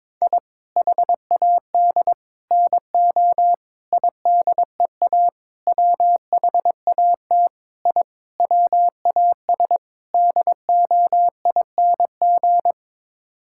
Morse code, heard as a keyed tone, about 21 wpm, 700 Hz. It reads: I HAD NO IDEA WHAT I WAS DOING
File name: practice / 22wpm / Sets of 8 Words - 22wpm